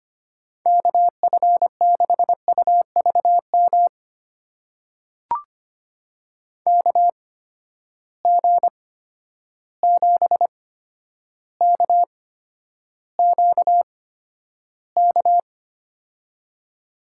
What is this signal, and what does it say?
Morse code, heard as a keyed tone, about 25 wpm, spaced as for 11 wpm, 700 Hz. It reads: KF6UVM KG7KQK
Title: USA Call Signs - 6 Characters - 8x Character Spacing - 25wpm